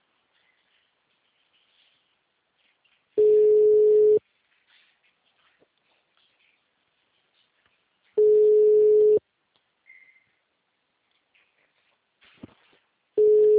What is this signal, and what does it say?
Ringback tone (Europe, 425 Hz).